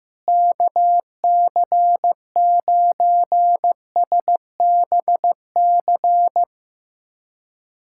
Morse code, heard as a keyed tone, about 15 wpm, 700 Hz. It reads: KC9SBC
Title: USA Callsign DE Callsign - 6 Characters - 15wpm